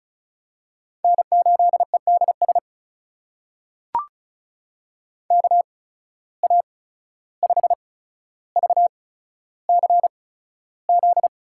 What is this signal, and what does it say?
Morse code, heard as a keyed tone, about 36 wpm, spaced as for 14 wpm, 700 Hz. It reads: N8EDS KA5VCZ
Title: USA Call Signs - 8x Character Spacing - 35wpm